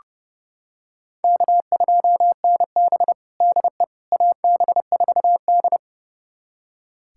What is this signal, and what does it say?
Morse code, heard as a keyed tone, about 32 wpm, 700 Hz. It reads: K2NB DE AB4D
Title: USA Callsign DE Callsign - 4 Characters - 30wpm